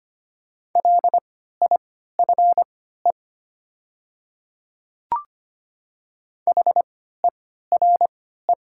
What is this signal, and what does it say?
Morse code, heard as a keyed tone, about 25 wpm, spaced as for 14 wpm, 700 Hz. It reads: LIFE HERE
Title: 3x Character Spacing - Top 300 Words - 25wpm